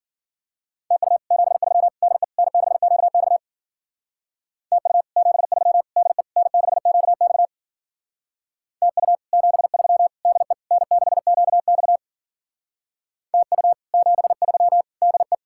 Morse code, heard as a keyed tone, about 48 wpm, 700 Hz. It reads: TU 73 DE N6QX TU 73 DE N6QX TU 73 DE N6QX TU 73 DE
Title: Speed-Racing Common QSO Phrases Encore - 40wpm